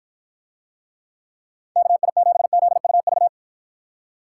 Morse code, heard as a keyed tone, about 53 wpm, 700 Hz. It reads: KI7ZUV